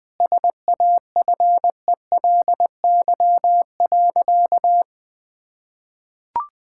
Morse code, heard as a keyed tone, about 20 wpm, 700 Hz. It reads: SAFELY.
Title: Easy Sentences with Hints - 20wpm